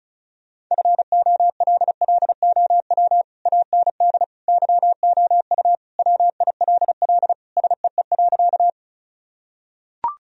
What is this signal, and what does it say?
Morse code, heard as a keyed tone, about 34 wpm, 700 Hz. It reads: FOLLOW AND YOU WILL SEE.
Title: Sentences from Top 1000 Words - Review Part 1 - 35wpm